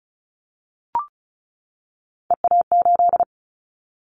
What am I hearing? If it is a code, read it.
Morse code, heard as a keyed tone, about 36 wpm, 700 Hz. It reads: EA8
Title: Callsign Prefixes for Top 100 DXCC - 35wpm